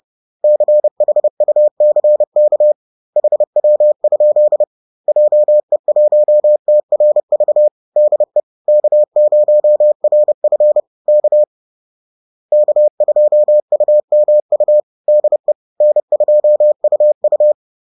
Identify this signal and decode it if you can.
Morse code, heard as a keyed tone, about 30 wpm, 600 Hz. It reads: CHUCK HW? JE1TRV DE K0RF K K2UMU DE N2UU
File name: 30wpm